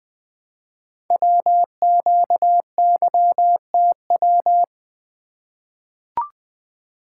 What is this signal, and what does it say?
Morse code, heard as a keyed tone, about 20 wpm, 700 Hz. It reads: WQYTW
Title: Code Groups - 5 Letters - 20wpm